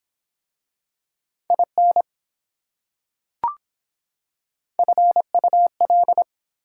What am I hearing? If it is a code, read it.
Morse code, heard as a keyed tone, about 25 wpm, 700 Hz. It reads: IN FUL